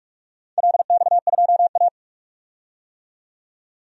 Morse code, heard as a keyed tone, about 44 wpm, 700 Hz. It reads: LX2A
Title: DXCC-100 Call Signs - 4 Characters - 45wpm